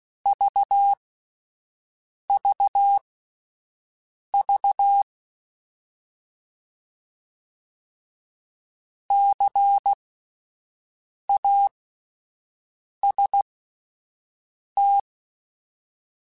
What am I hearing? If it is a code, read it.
Morse code, heard as a keyed tone, about 16 wpm, spaced as for 5 wpm, 800 Hz. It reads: VVV CAST
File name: Alphabetical / Alphabetical mp3s / Cs @5